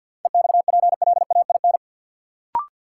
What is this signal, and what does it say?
Morse code, heard as a keyed tone, about 53 wpm, 700 Hz. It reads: EXPLAIN
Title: Sentences from Top 1100 Words - New Word Drill - 50wpm